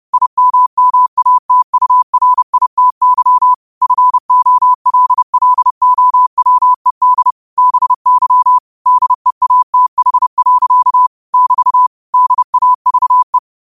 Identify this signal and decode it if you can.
Morse code, heard as a keyed tone, about 30 wpm, 1000 Hz. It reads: IMMATURITY FOLLOWED BY DEATH. <BT> DAVE
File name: qotdfiles / 30wpm